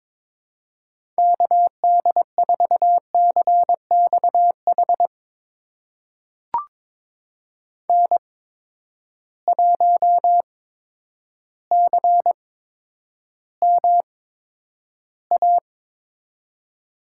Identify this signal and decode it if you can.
Morse code, heard as a keyed tone, about 22 wpm, spaced as for 10 wpm, 700 Hz. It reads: KD4CXH N1CMA
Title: USA Call Signs - 8x Character Spacing - 22wpm